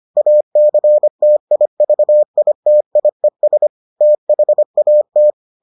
Morse code, heard as a keyed tone, about 25 wpm, 600 Hz. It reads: ACTIVITIES THAT